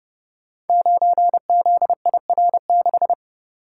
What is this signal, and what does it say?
Morse code, heard as a keyed tone, about 30 wpm, 700 Hz. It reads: 9ZIR6
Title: Code Groups - 5 Letter:Numbers - 30wpm